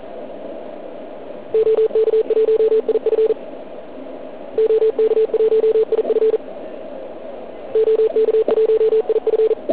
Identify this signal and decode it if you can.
Morse code, heard as a keyed tone, about 40 wpm, 440 Hz. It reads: OK1IF OK1IF OK1IF